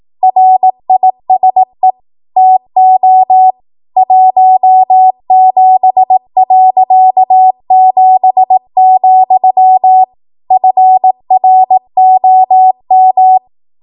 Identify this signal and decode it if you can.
Morse code, heard as a keyed tone, about 18 wpm, 750 Hz. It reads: RISE TO 17.7, FROM
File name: Archive / 18 WPM / 18WPM